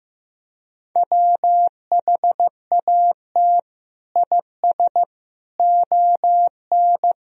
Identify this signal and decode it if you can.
Morse code, heard as a keyed tone, about 16 wpm, 700 Hz. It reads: WHAT IS ON
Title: Sentences from Top 1000 Words - Review Part 4 - 15wpm